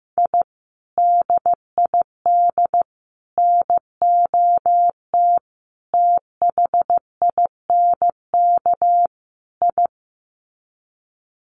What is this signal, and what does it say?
Morse code, heard as a keyed tone, about 15 wpm, 700 Hz. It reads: I DID NOT THINK I
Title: Sets of 5 Words Encore 15wpm